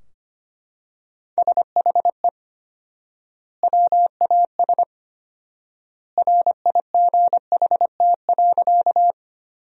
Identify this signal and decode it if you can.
Morse code, heard as a keyed tone, about 25 wpm, 700 Hz. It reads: SHE WAS RIGHT.
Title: 4x Word-Spacing - Sentences from Top 200 Words - 25wpm